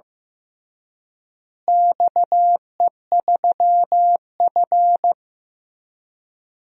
Morse code, heard as a keyed tone, about 15 wpm, 700 Hz. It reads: XE3F
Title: DXCC-100 Call Signs - 4 Characters - Repeated 2x - 15wpm